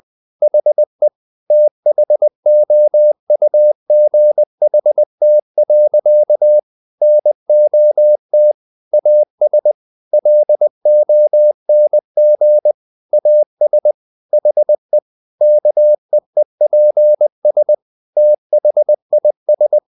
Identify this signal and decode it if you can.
Morse code, heard as a keyed tone, about 20 wpm, 600 Hz. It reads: HE THOUGHT. NOT AS LONG AS HE KEEPS THIS
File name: the old man and the sea / the old man and the sea part 1 20wpm